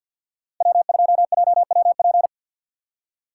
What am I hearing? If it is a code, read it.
Morse code, heard as a keyed tone, about 49 wpm, 700 Hz. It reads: W2JWP
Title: USA Call Signs - 5 Characters - 2x Character Spacing - 50wpm